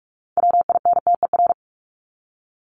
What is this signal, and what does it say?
Morse code, heard as a keyed tone, about 44 wpm, 700 Hz. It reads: WINTER
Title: Top 1501-1750 Words 45wpm